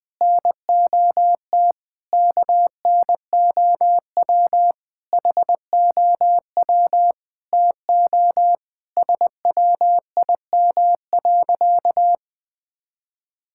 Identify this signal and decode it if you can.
Morse code, heard as a keyed tone, about 20 wpm, 700 Hz. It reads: NOT KNOW HOW TO SWIM.
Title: Sentences with 5-Word Phrases - 20wpm